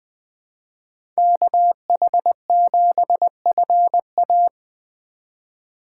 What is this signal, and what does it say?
Morse code, heard as a keyed tone, about 20 wpm, 700 Hz. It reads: KH7FA
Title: USA Call Signs - 5 Characters - 2x Character Spacing - 20wpm